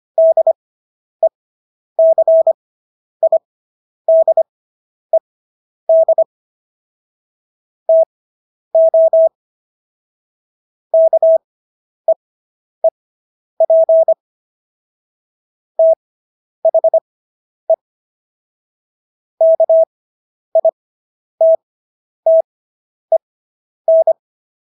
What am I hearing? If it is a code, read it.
Morse code, heard as a keyed tone, about 25 wpm, spaced as for 10 wpm, 650 Hz. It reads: DECIDED TO KEEP THE KITTEN